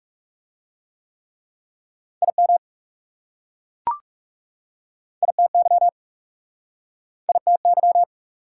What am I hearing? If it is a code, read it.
Morse code, heard as a keyed tone, about 41 wpm, 690 Hz. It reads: IM ITY ITY